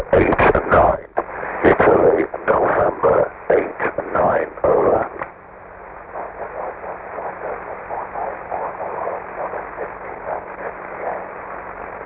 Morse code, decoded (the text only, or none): none